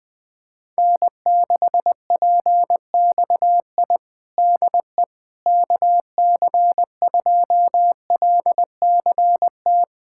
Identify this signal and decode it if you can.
Morse code, heard as a keyed tone, about 20 wpm, 700 Hz. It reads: N6PXI DE KC2LCT
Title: USA Callsign DE Callsign 20wpm